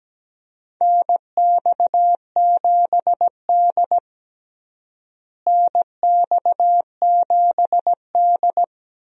Morse code, heard as a keyed tone, about 17 wpm, 700 Hz. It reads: NX7D NX7D
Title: USA Call Signs - 4 Characters - Repeated 2x - 17wpm